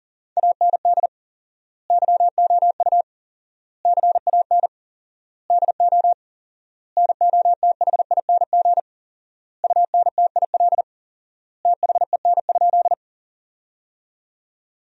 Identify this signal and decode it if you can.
Morse code, heard as a keyed tone, about 40 wpm, 700 Hz. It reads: AND YOU CAN DO NOTHING UNTIL THEN?
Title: 4x Word-Spacing - Sentences from Top 500 Words - 40wpm